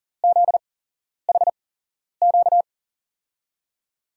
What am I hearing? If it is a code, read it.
Morse code, heard as a keyed tone, about 42 wpm, spaced as for 11 wpm, 700 Hz. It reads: ZHQ
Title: USA Call Signs - 6 Characters - 8x Character Spacing - 40wpm